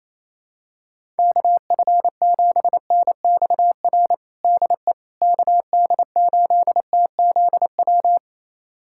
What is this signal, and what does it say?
Morse code, heard as a keyed tone, about 27 wpm, 700 Hz. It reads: KF7NXR DE KD8TZW
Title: USA Callsign DE Callsign - 6 Characters - 28wpm